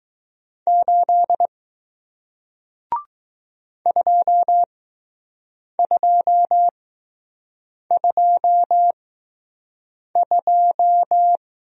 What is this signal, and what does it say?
Morse code, heard as a keyed tone, about 20 wpm, spaced as for 6 wpm, 700 Hz. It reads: 8 2222